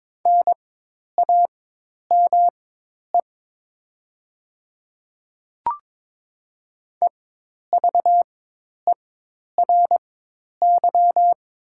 Morse code, heard as a keyed tone, about 22 wpm, spaced as for 10 wpm, 700 Hz. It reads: NAME EVERY
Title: 4x Character Spacing - Top 300 Words - 22wpm